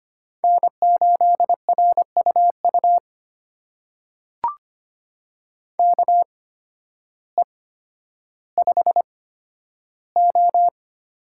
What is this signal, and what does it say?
Morse code, heard as a keyed tone, about 25 wpm, spaced as for 12 wpm, 700 Hz. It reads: N8RUU KE5O
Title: USA Call Signs - 8x Character Spacing - 25wpm